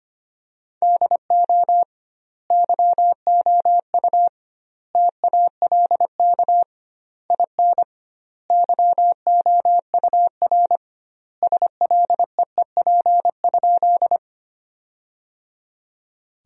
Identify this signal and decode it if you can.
Morse code, heard as a keyed tone, about 25 wpm, 700 Hz. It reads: DO YOU TALK IN YOUR SLEEP?
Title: 2x Word-Spacing - Sentences from Top 500 Words - 25wpm